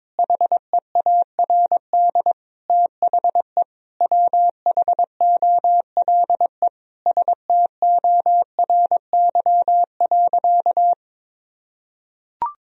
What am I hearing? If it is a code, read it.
Morse code, heard as a keyed tone, about 22 wpm, 700 Hz. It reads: HEARD THE WHOLE STORY.
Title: Sentences from Top 1000 Words - Review Part 3 - 22wpm